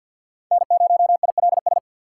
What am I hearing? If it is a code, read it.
Morse code, heard as a keyed tone, about 50 wpm, 700 Hz. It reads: N0ILS
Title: USA Call Signs - 5 Characters - 2x Character Spacing - 50wpm